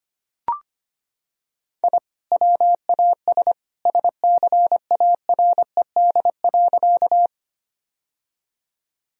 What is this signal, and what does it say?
Morse code, heard as a keyed tone, about 25 wpm, 700 Hz. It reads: I WAS SCARED.
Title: Sentences from Top 1000 Words - Review Part 4 - 25wpm